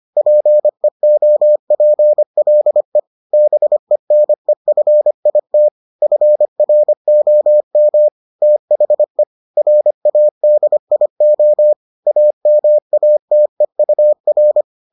Morse code, heard as a keyed tone, about 25 wpm, 600 Hz. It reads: PEOPLE BENEFIT FROM THE RADIO AMATEUR